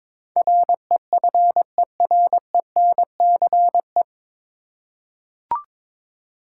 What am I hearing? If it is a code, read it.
Morse code, heard as a keyed tone, about 21 wpm, 700 Hz. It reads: REFERENCE
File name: 9 Letter Words - First 3 Letters Given - 22wpm